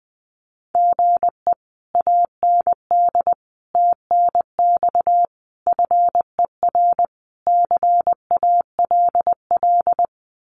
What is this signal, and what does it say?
Morse code, heard as a keyed tone, about 20 wpm, 700 Hz. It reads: GE AND TNX FER CALL
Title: Common QSO Phrases Encore 20wpm